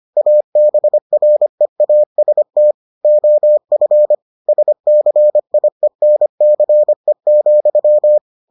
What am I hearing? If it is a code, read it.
Morse code, heard as a keyed tone, about 25 wpm, 600 Hz. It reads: ABREAST OF SCIENCE,